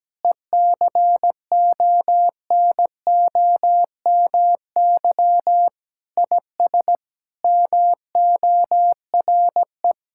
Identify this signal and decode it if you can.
Morse code, heard as a keyed tone, about 17 wpm, 700 Hz. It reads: ECONOMY IS MORE
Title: Sentences from Top 1000 Words - Review Part 6 - 17wpm